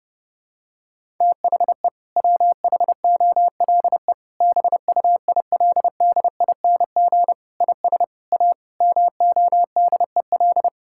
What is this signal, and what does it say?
Morse code, heard as a keyed tone, about 30 wpm, 700 Hz. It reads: THE WHOLE BUILDING IS A MODEL
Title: Sentences from Top 1000 Words - Review Part 5 - 30wpm